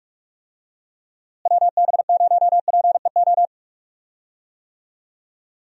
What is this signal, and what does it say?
Morse code, heard as a keyed tone, about 46 wpm, 700 Hz. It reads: WB0PEO
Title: USA Call Signs - 6 Characters - Repeated 2x - 45wpm